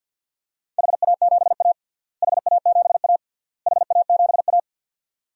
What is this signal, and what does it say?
Morse code, heard as a keyed tone, about 51 wpm, 700 Hz. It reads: HA7A HA7A HA7A
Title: DXCC-100 Call Signs - 4 Characters - Repeated 3x - 50wpm